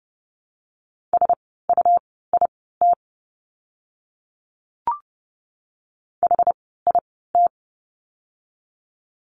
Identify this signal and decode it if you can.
Morse code, heard as a keyed tone, about 30 wpm, spaced as for 17 wpm, 700 Hz. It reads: SUIT HIT
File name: Lesson 17f - Letter u - Words 30wpm